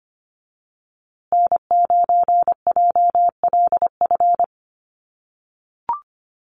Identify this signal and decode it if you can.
Morse code, heard as a keyed tone, about 25 wpm, 700 Hz. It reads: N9JLF
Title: USA Call Signs Repeated 2x 25wpm